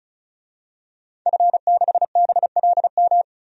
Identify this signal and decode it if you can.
Morse code, heard as a keyed tone, about 36 wpm, 700 Hz. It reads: F6BLM
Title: DXCC-100 Call Signs - 5 Characters - 35wpm